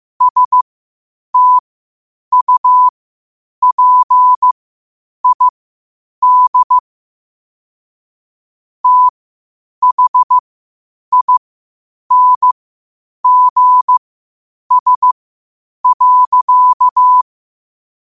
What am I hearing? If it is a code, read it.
Morse code, heard as a keyed tone, about 15 wpm, spaced as for 8 wpm, 1000 Hz. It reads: STUPID THINGS.
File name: qotdfiles / 8wpm